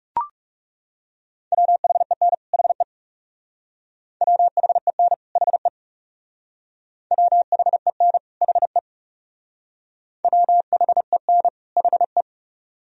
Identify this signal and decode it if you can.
Morse code, heard as a keyed tone, about 40 wpm, 700 Hz. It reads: WHEN HE WHEN HE WHEN HE WHEN HE